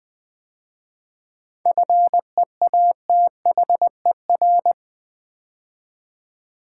Morse code, heard as a keyed tone, about 20 wpm, 700 Hz. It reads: FEATHER
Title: 7 Letter Words - First 3 Letters Given - 20wpm